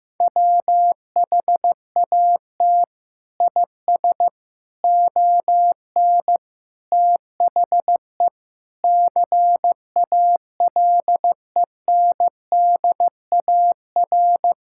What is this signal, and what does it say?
Morse code, heard as a keyed tone, about 14 wpm, 700 Hz. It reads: WHAT IS ON THE CALENDAR
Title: Sentences from Top 1600 Words 15wpm